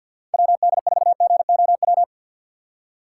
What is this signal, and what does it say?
Morse code, heard as a keyed tone, about 49 wpm, 700 Hz. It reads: WD4GOW
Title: USA Call Signs - 6x Character Spacing - 50wpm